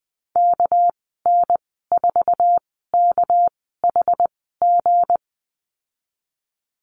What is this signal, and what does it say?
Morse code, heard as a keyed tone, about 20 wpm, spaced as for 15 wpm, 700 Hz. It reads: KN4KHG
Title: 213 Lesson 34g - Number 8 - Call Signs 20wpm